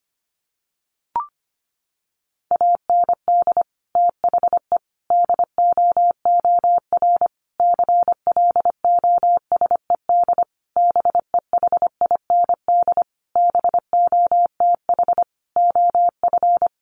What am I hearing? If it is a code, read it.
Morse code, heard as a keyed tone, about 25 wpm, 700 Hz. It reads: AND THE DOOR CLOSED BEHIND BOTH OF
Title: Sentences from Top 1400 Words 25wpm